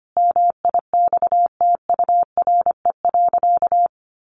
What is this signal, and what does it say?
Morse code, heard as a keyed tone, about 25 wpm, 700 Hz. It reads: MIXTURE.
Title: Sentences from Top 1500 Words 25wpm